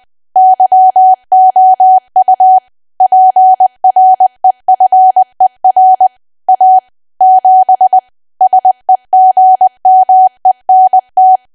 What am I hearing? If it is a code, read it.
Morse code, heard as a keyed tone, about 20 wpm, 750 Hz. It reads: YOU PREFER A 7 SEGMENT
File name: cwa / W1AW-Level3 / 20WPM